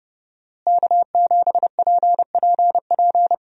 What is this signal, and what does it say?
Morse code, heard as a keyed tone, about 32 wpm, 700 Hz. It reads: K7PPP
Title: USA Call Signs - 5 Characters - Repeated 2x - 30wpm